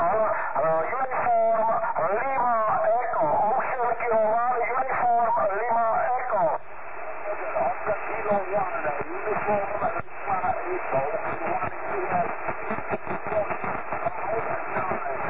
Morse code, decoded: none